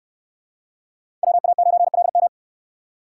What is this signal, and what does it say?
Morse code, heard as a keyed tone, about 69 wpm, 700 Hz. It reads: WA9LM